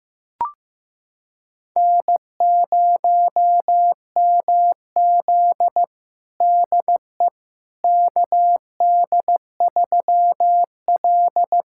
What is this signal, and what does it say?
Morse code, heard as a keyed tone, about 15 wpm, 700 Hz. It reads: N0MZ DE KD3L
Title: USA Callsign DE Callsign - 4 Characters - 15wpm